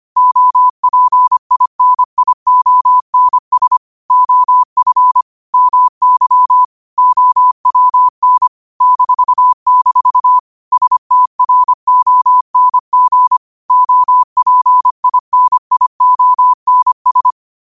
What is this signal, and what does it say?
Morse code, heard as a keyed tone, about 25 wpm, 1000 Hz. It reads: OPINIONS OF MY OWN -- STRONG OPINIONS
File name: qotdfiles / 25wpm